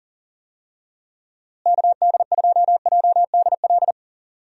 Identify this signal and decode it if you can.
Morse code, heard as a keyed tone, about 39 wpm, 700 Hz. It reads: KD2JDL
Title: USA Call Signs - 6 Characters - Repeated 3x - 40wpm